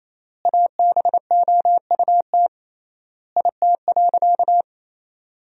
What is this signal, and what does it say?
Morse code, heard as a keyed tone, about 28 wpm, 700 Hz. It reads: ABOUT IT.
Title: 3x Word-Spacing - Sentences from Top 500 Words - 28wpm